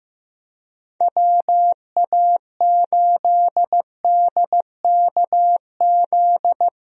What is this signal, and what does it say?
Morse code, heard as a keyed tone, about 15 wpm, 700 Hz. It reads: WA8DKZ
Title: USA Call Signs - 6 Characters - 15wpm